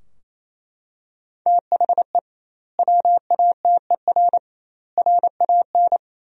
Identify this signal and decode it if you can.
Morse code, heard as a keyed tone, about 28 wpm, 700 Hz. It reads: THE WATER RAN
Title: 2x Word-Spacing - Sentences from Top 500 Words - 28wpm